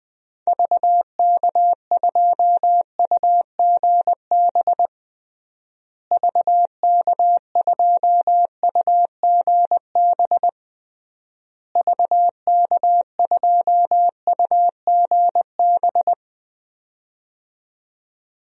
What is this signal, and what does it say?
Morse code, heard as a keyed tone, about 20 wpm, 700 Hz. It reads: VK2UGB VK2UGB VK2UGB
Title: DXCC-100 Call Signs - 6 Characters - Repeated 3x - 20wpm